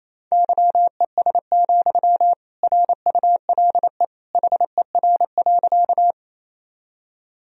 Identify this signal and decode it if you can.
Morse code, heard as a keyed tone, about 28 wpm, 700 Hz. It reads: YES, RULE HER.
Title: Sentences from Top 1000 Words - Review Part 2 - 28wpm